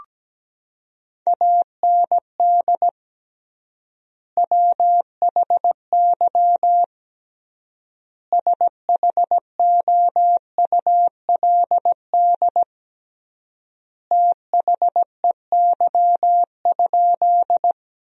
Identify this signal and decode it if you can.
Morse code, heard as a keyed tone, about 17 wpm, 700 Hz. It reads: AND WHY SHOULD THEY?
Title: 3x Word-Spacing - Sentences from Top 300 Words - 17wpm